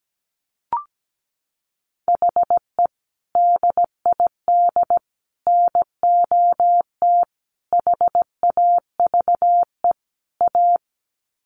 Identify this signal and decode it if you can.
Morse code, heard as a keyed tone, about 17 wpm, 700 Hz. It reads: HE DID NOT HAVE A
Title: Sets of 5 Words Encore 17wpm